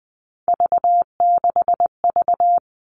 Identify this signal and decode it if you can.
Morse code, heard as a keyed tone, about 20 wpm, 700 Hz. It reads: V6V
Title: All Cut Numbers - 3-Digit 20wpm